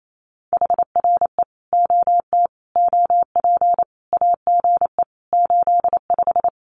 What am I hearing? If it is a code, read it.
Morse code, heard as a keyed tone, about 28 wpm, 700 Hz. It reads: HRE OT OP AGE 85
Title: Common QSO Phrases 28wpm